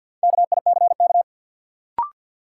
Morse code, heard as a keyed tone, about 52 wpm, 700 Hz. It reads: KICK
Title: Sentences from Top 1300 Words - New Word Drill - 50wpm